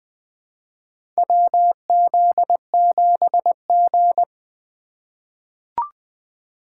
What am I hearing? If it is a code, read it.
Morse code, heard as a keyed tone, about 20 wpm, 700 Hz. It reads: WZ7G
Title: USA Call Signs - 4x Character Spacing - 20wpm